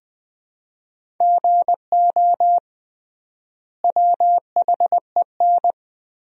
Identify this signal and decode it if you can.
Morse code, heard as a keyed tone, about 20 wpm, 700 Hz. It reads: GO WHEN
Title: 3x Word-Spacing - Sentences from Top 500 Words - 20wpm